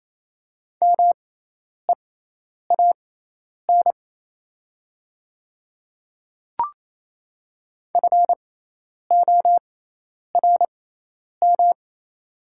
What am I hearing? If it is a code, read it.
Morse code, heard as a keyed tone, about 28 wpm, spaced as for 10 wpm, 700 Hz. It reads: MEAN FORM
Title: 6x Character Spacing - Top 300 Words - 28wpm